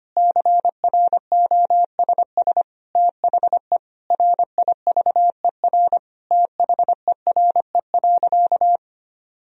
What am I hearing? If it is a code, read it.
Morse code, heard as a keyed tone, about 25 wpm, 700 Hz. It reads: CROSS THE RIVER THERE.
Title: Sentences from Top 1000 Words - Review Part 3 - 25wpm